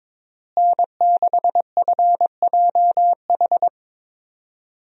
Morse code, heard as a keyed tone, about 22 wpm, 700 Hz. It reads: N6FJH
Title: USA Call Signs - 5 Characters - Repeated 3x - 22wpm